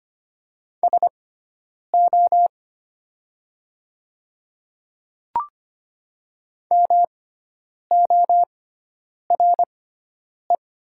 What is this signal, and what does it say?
Morse code, heard as a keyed tone, about 25 wpm, spaced as for 9 wpm, 700 Hz. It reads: SO MORE
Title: 6x Character Spacing - Top 100 Words - 25wpm